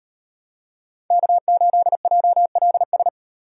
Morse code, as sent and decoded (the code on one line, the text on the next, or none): -.- ---.. .--- .-.. ...
K8JLS